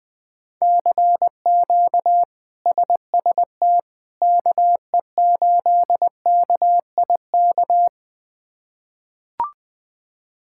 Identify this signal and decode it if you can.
Morse code, heard as a keyed tone, about 20 wpm, 700 Hz. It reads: CQ SST KE8KIK
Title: K1USN SST QSO Practice - 20wpm